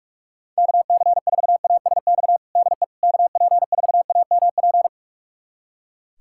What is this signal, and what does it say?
Morse code, heard as a keyed tone, about 41 wpm, 700 Hz. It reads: KC4ASX DE KP4AMP
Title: USA Callsign DE Callsign 45wpm